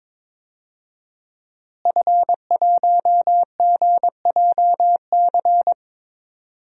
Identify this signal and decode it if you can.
Morse code, heard as a keyed tone, about 21 wpm, 700 Hz. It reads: F1GJC
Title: DXCC-100 Call Signs - 5 Characters - Repeated 3x - 22wpm